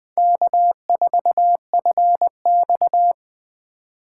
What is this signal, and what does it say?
Morse code, heard as a keyed tone, about 20 wpm, 700 Hz. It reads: K4FX